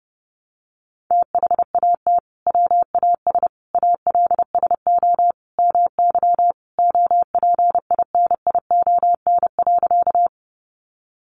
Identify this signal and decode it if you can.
Morse code, heard as a keyed tone, about 30 wpm, 700 Hz. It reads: THAT WAS ALSO MY OPINION.
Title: Sentences from Top 1200 Words 30wpm